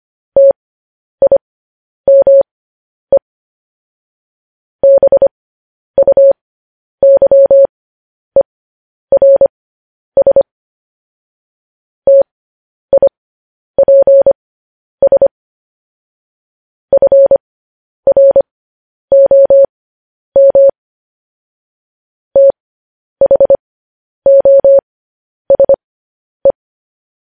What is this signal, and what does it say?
Morse code, heard as a keyed tone, about 25 wpm, spaced as for 10 wpm, 550 Hz. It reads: TIME BUYERS TIPS FROM THOSE
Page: news-headlines-in-morse-code-at-10-wpm